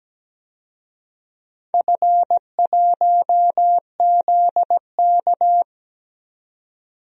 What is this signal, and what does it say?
Morse code, heard as a keyed tone, about 17 wpm, 700 Hz. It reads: F1ZK
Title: DXCC-100 Call Signs - 4 Characters - 17wpm